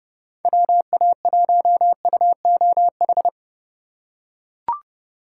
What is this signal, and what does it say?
Morse code, heard as a keyed tone, about 29 wpm, 700 Hz. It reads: WA1UOH